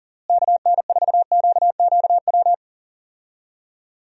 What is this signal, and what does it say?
Morse code, heard as a keyed tone, about 40 wpm, 700 Hz. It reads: KN4QQW
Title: USA Call Signs - 6 Characters - Repeated 2x - 40wpm